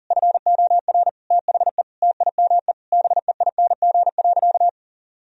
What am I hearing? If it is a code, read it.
Morse code, heard as a keyed tone, about 40 wpm, 700 Hz. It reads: FOR THE TIME BEING.
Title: Sentences from Top 1000 Words - Review Part 4 - 40wpm